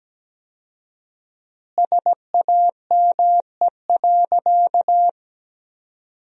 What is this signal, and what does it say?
Morse code, heard as a keyed tone, about 17 wpm, 700 Hz. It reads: SAME.
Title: 4x Word-Spacing - Sentences from Top 300 Words - 17wpm